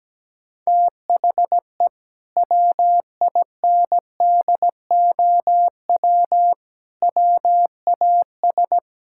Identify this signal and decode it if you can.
Morse code, heard as a keyed tone, about 17 wpm, 700 Hz. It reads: THE WINDOW WAS
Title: Sentences from Top 1600 Words 17wpm